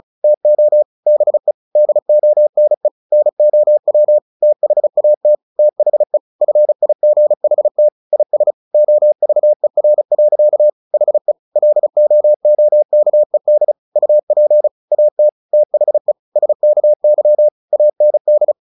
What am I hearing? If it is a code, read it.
Morse code, heard as a keyed tone, about 34 wpm, 600 Hz. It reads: TO BE DONE NOW THAT THE FIGHT IS OVER. HE LOOKED UP AT THE SKY AND
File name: the old man and the sea 2 35wpm